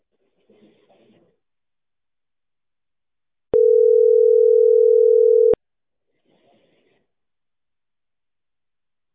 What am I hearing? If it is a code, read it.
Ringback tone (North America), 440 + 480 Hz.